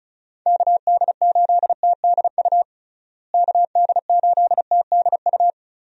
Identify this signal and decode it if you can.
Morse code, heard as a keyed tone, about 34 wpm, 700 Hz. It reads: KD8TDU KD8TDU
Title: USA Call Signs - 6 Characters - Repeated 3x - 35wpm